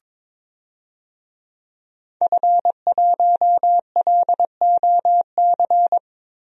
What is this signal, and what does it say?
Morse code, heard as a keyed tone, about 22 wpm, 700 Hz. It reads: F1LOC
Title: DXCC-100 Call Signs - 5 Characters - Repeated 3x - 22wpm